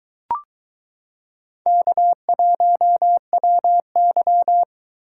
Morse code, heard as a keyed tone, about 23 wpm, 700 Hz. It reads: K1WY